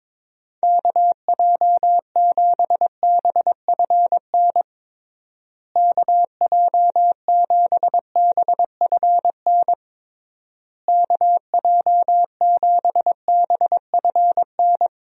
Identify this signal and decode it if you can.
Morse code, heard as a keyed tone, about 22 wpm, 700 Hz. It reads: KJ7BFN KJ7BFN KJ7BFN
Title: USA Call Signs - 6 Characters - Repeated 3x - 22wpm